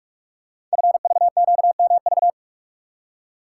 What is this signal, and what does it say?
Morse code, heard as a keyed tone, about 44 wpm, 700 Hz. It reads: FVQMV
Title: Code Groups - 5 Letter:Numbers - 45wpm